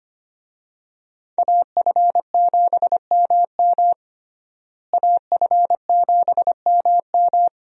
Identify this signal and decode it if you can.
Morse code, heard as a keyed tone, about 25 wpm, 700 Hz. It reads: AF7MM AF7MM
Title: USA Call Signs - 5 Characters - Repeated 2x - 25wpm